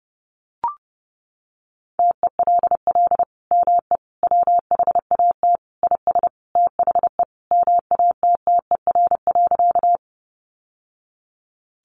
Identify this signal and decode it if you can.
Morse code, heard as a keyed tone, about 30 wpm, 700 Hz. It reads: TELL ME WHAT IS THE MATTER.
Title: Sentences from Top 700 Words 30wpm